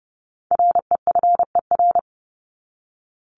Morse code, heard as a keyed tone, about 30 wpm, 700 Hz. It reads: REFER
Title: Top 751-1000 Words 30wpm